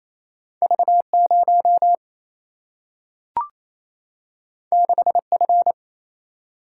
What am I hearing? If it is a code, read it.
Morse code, heard as a keyed tone, about 28 wpm, 700 Hz. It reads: V 0 6 F
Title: Single-Letter Single-Number 28wpm